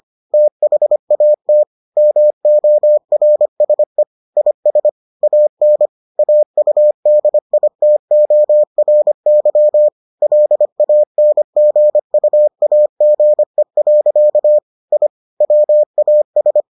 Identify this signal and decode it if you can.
Morse code, heard as a keyed tone, about 25 wpm, 600 Hz. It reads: THAT MORSE IS AN AUDITORY LANGUAGE. I WAS